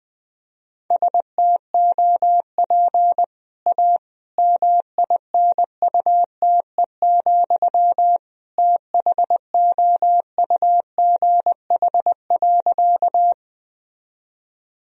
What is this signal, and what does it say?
Morse code, heard as a keyed tone, about 20 wpm, 700 Hz. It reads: STOP A MINUTE, THOUGH.
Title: Sentences from Top 1000 Words - Review Part 2 - 20wpm